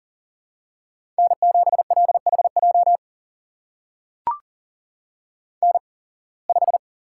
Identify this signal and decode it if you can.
Morse code, heard as a keyed tone, about 40 wpm, spaced as for 26 wpm, 700 Hz. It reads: N7LHJ N5
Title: USA Call Signs - 5 Characters - 8x Character Spacing - 40wpm